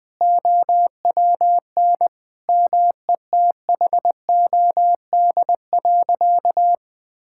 Morse code, heard as a keyed tone, about 20 wpm, 700 Hz. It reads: OWN METHOD.
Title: Sentences from Top 1000 Words - Review Part 6 - 20wpm